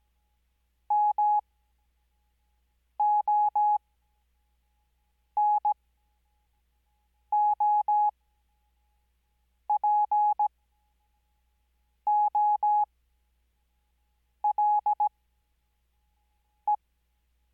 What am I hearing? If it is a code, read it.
Morse code, heard as a keyed tone, about 17 wpm, spaced as for 5 wpm, 840 Hz. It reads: MONOPOLE
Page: Farnsworth Spacing: 15WPM Character speed Spaced at 5WPM.